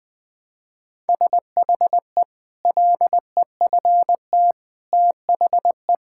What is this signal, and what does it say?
Morse code, heard as a keyed tone, about 20 wpm, 700 Hz. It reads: SHE LEFT THE
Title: Sentences from Top 1600 Words 20wpm